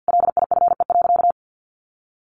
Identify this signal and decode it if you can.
Morse code, heard as a keyed tone, about 50 wpm, 700 Hz. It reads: LIFE.
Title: Sentences from Top 1400 Words 50wpm